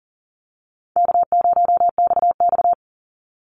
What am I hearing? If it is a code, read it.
Morse code, heard as a keyed tone, about 42 wpm, 700 Hz. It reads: K0XX
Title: Easy Call Signs - 40wpm